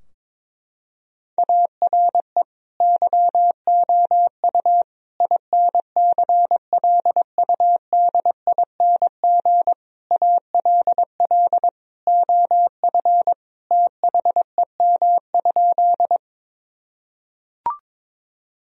Morse code, heard as a keyed tone, about 22 wpm, 700 Hz. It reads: ARE YOU INCLUDING ALL OF THEM?
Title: Sentences from Top 1000 Words - Review Part 4 - 22wpm